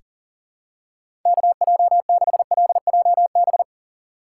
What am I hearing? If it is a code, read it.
Morse code, heard as a keyed tone, about 37 wpm, 700 Hz. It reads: KJ6LJB
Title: USA Call Signs - 6 Characters - 3x Character Spacing - 40wpm